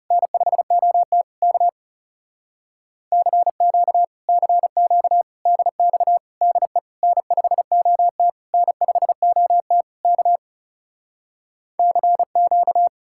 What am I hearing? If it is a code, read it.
Morse code, heard as a keyed tone, about 40 wpm, 700 Hz. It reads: N5OT K CQ CQ DX DE N5OT N5OT K CQ